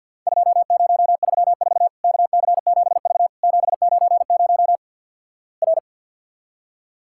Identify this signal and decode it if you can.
Morse code, heard as a keyed tone, about 48 wpm, 700 Hz. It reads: 2034 KC7V 790 R
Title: Worked All Europe DX Contest - QTC Exchange - 50wpm